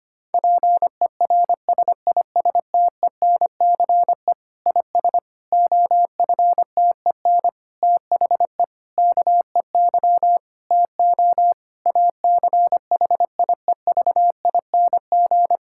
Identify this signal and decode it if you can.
Morse code, heard as a keyed tone, about 25 wpm, 700 Hz. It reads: PERSISTENCE IS OFTEN THE KEY TO ACHIEVING
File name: practice / 25wpm / Difficult Sentences with Hints - 25wpm